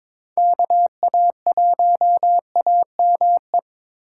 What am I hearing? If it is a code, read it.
Morse code, heard as a keyed tone, about 26 wpm, 700 Hz. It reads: KA1AME
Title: Speed Racing - USA 2x3 Call Signs 22wpm